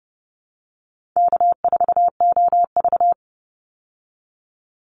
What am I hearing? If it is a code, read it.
Morse code, heard as a keyed tone, about 30 wpm, 700 Hz. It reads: K4OV
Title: Easy Call Signs Repeated 3x - 30wpm